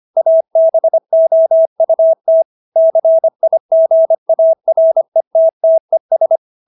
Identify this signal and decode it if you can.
Morse code, heard as a keyed tone, about 25 wpm, 650 Hz. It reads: ABOUT CIGARETTES